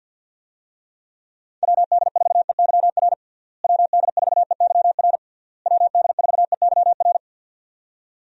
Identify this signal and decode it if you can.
Morse code, heard as a keyed tone, about 49 wpm, 700 Hz. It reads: WD4EYR WD4EYR WD4EYR
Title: USA Call Signs - 6 Characters - Repeated 3x - 50wpm